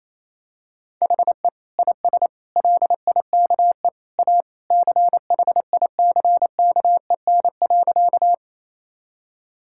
Morse code, heard as a keyed tone, about 28 wpm, 700 Hz. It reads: HE IS LIKE A CHICKEN.
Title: Sentences from Top 1000 Words - Review Part 5 - 28wpm